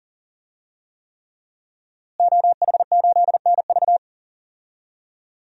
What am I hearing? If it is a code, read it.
Morse code, heard as a keyed tone, about 37 wpm, 700 Hz. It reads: OH8NV
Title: DXCC-100 Call Signs - 5 Characters - Repeated 3x - 40wpm